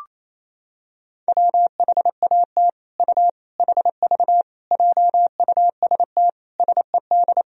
Morse code, heard as a keyed tone, about 28 wpm, 700 Hz. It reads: WHAT U HV JUST SED
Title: Sentences from Top 200 Words and Abbreviations 28wpm